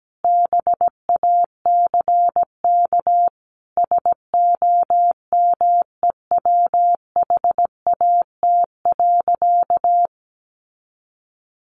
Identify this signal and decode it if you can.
Morse code, heard as a keyed tone, about 17 wpm, 700 Hz. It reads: BACK SOMEWHAT.
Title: Sentences from Top 1300 Words 17wpm